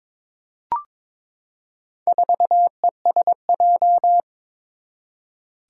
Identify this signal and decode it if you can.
Morse code, heard as a keyed tone, about 22 wpm, 700 Hz. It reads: 4ESJ
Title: Four Challenging Letter-Number Combinations 22wpm